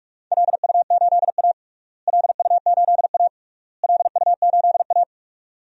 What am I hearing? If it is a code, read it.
Morse code, heard as a keyed tone, about 46 wpm, 700 Hz. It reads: LU8A LU8A LU8A
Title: DXCC-100 Call Signs - 4 Characters - Repeated 3x - 45wpm